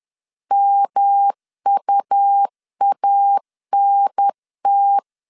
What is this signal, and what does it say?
Morse code, heard as a keyed tone, about 10 wpm, 800 Hz. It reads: MUANT
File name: CW / DET / 1 3 1500C S B 10wpm 3335